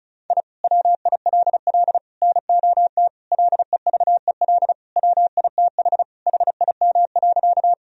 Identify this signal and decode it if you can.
Morse code, heard as a keyed tone, about 34 wpm, 700 Hz. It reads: I WILL NOT LEVEL WITH HIM.
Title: Sentences from Top 1000 Words - Review Part 4 - 35wpm